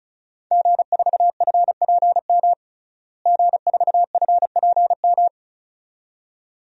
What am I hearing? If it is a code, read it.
Morse code, heard as a keyed tone, about 36 wpm, 700 Hz. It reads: G4FPM G4FPM
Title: DXCC-100 Call Signs - 5 Characters - Repeated 2x - 35wpm